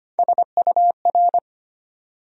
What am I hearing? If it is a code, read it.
Morse code, heard as a keyed tone, about 25 wpm, 700 Hz. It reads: SUR
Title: USA Call Sign Postfixes - 3 Letters 25wpm